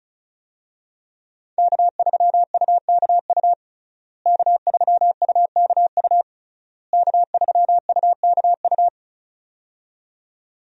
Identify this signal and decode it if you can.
Morse code, heard as a keyed tone, about 34 wpm, 700 Hz. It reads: K3UKU K3UKU K3UKU
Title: USA Call Signs - 5 Characters - Repeated 3x - 35wpm